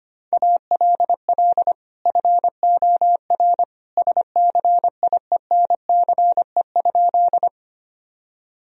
Morse code, heard as a keyed tone, about 25 wpm, 700 Hz. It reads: ALL FOR SCIENCE?
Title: Sentences from Top 1000 Words - Review Part 2 - 25wpm